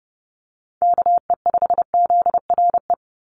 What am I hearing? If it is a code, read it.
Morse code, heard as a keyed tone, about 30 wpm, 700 Hz. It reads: KE5ZRE
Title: USA Call Signs 30wpm